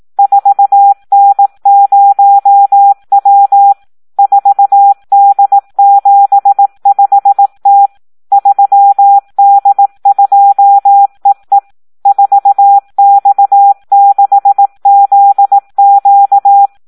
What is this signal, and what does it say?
Morse code, heard as a keyed tone, about 18 wpm, 800 Hz. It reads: 4N0W 4D75T 3D2EE 4X6ZQ
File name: Audio practica CW / SD 1 18